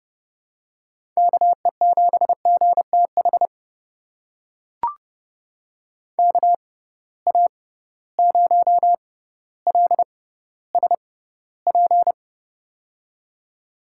Morse code, heard as a keyed tone, about 32 wpm, spaced as for 16 wpm, 700 Hz. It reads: KE7GTH KA0LSP
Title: USA Call Signs - 6 Characters - 6x Character Spacing - 30wpm